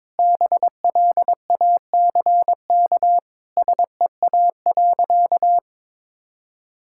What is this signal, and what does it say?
Morse code, heard as a keyed tone, about 22 wpm, 700 Hz. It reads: BLACK SEA.
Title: Sentences from Top 1000 Words - Review Part 5 - 22wpm